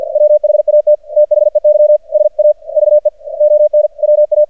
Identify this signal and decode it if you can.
Morse code, heard as a keyed tone, about 48 wpm, 600 Hz. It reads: WHAT THEY SAVE ON WA
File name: archived / CRSnet